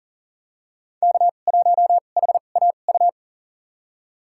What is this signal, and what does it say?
Morse code, heard as a keyed tone, about 40 wpm, spaced as for 29 wpm, 700 Hz. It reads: K1HAU
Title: USA Call Signs - 5 Characters - 2x Character Spacing - 40wpm